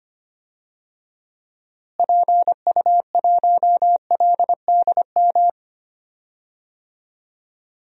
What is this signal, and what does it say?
Morse code, heard as a keyed tone, about 25 wpm, 700 Hz. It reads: PU1LDM
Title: DXCC-100 Call Signs - 6 Characters - Repeated 3x - 25wpm